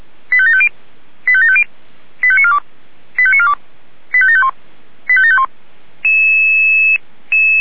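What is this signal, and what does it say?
Five-tone selective call (ZVEI), heard as quick runs of five tones, tones of about 70 ms. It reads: 85750 85750 85942 85942 85751 85751
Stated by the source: ZVEI five-tone call received by SDR, from feuerwehr-orlamuende.org.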